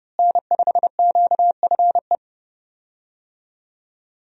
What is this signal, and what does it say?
Morse code, heard as a keyed tone, about 30 wpm, 700 Hz. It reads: N5QFE
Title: USA Call Signs - 5 Characters - Repeated 3x - 30wpm